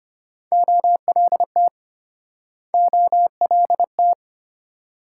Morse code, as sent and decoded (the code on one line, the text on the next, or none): --- .-.. - / --- .-.. -
OLT OLT